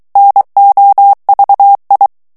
Morse code, heard as a keyed tone, about 23 wpm, 800 Hz. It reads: NOVI
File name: morse mp3 / ITALIEN texte clair 1 vitesse 22 mots minute